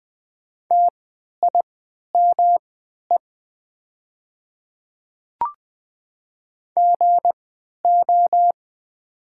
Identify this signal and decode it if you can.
Morse code, heard as a keyed tone, about 20 wpm, spaced as for 11 wpm, 700 Hz. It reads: TIME GO
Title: 3x Character Spacing - Top 300 Words - 20wpm